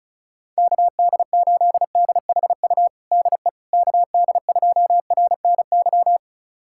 Morse code, heard as a keyed tone, about 36 wpm, 700 Hz. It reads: KD8DHU DE KD2RNY
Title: USA Callsign DE Callsign 35wpm